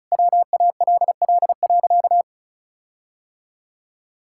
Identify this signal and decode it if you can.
Morse code, heard as a keyed tone, about 34 wpm, 700 Hz. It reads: WALL.
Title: Sentences from Top 1600 Words 35wpm